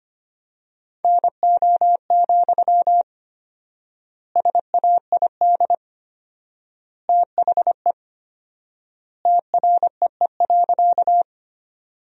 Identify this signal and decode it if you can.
Morse code, heard as a keyed tone, about 25 wpm, 700 Hz. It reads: NO, SAID THE TREE.
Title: 4x Word-Spacing - Sentences from Top 300 Words - 25wpm